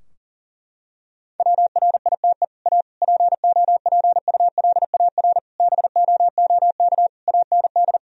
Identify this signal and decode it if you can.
Morse code, heard as a keyed tone, about 40 wpm, 700 Hz. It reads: WRITE A POPULAR BOOK AND
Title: Sentences from Top 1000 Words - Review Part 3 - 40wpm